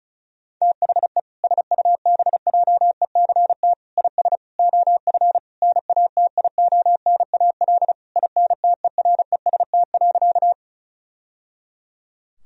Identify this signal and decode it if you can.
Morse code, heard as a keyed tone, about 34 wpm, 700 Hz. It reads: THE SUBJECT IS OF NATIONAL INTEREST.
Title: Sentences from Top 1000 Words - Review Part 4 - 35wpm